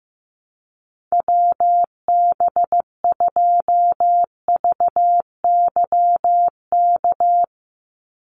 Morse code, heard as a keyed tone, about 15 wpm, 700 Hz. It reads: WB2VYK
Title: Call Signs Repeated 2x - 15wpm